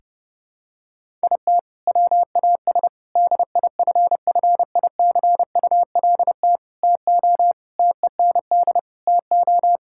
Morse code, heard as a keyed tone, about 32 wpm, 700 Hz. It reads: IT WAS DIFFICULT TO TEND TO
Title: Sentences from Top 1000 Words - Review Part 5 - 30wpm